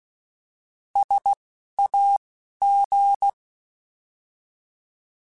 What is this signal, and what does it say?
Morse code, heard as a keyed tone, about 16 wpm, spaced as for 11 wpm, 800 Hz. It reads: SAG
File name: Alphabetical / Alphabetical mp3s / Ss @10